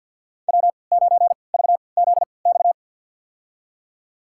Morse code, heard as a keyed tone, about 51 wpm, spaced as for 28 wpm, 690 Hz. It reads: W9VZX
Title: USA Call Signs - 5 Characters - 3x Character Spacing - 50wpm